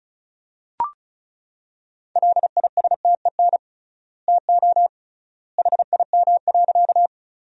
Morse code, heard as a keyed tone, about 36 wpm, 700 Hz. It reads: LISTEN TO HIM.
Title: 3x Word-Spacing - Sentences from Top 400 Words - 35wpm